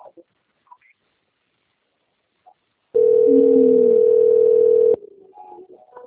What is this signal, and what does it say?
Ringback tone (North America), 440 + 480 Hz.
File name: out- - - - - .19691